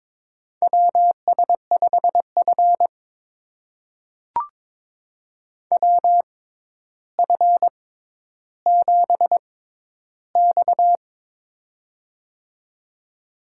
Morse code, heard as a keyed tone, about 22 wpm, spaced as for 11 wpm, 700 Hz. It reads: WS5F WF7X
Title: USA Call Signs - 4 Characters - 6x Character Spacing - 22wpm